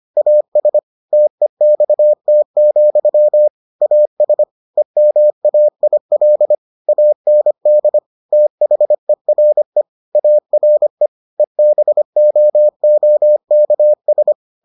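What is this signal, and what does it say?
Morse code, heard as a keyed tone, about 25 wpm, 600 Hz. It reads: AS TEXT, AS EMAIL AND THERE ARE EBOOKS
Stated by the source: Morse edition of the Foundations of Amateur Radio episodes.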